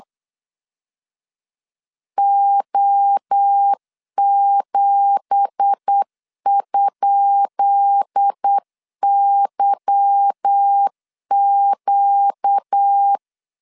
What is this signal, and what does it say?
Morse code, heard as a keyed tone, about 8 wpm, 800 Hz. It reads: O7?YQ
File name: LEZIONI / 10 SIGNS / 10B INT 8wpm 3534